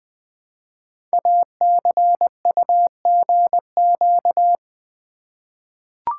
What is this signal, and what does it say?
Morse code, heard as a keyed tone, about 20 wpm, 700 Hz. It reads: ACUGQ
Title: Code Groups - 5 Letters - 20wpm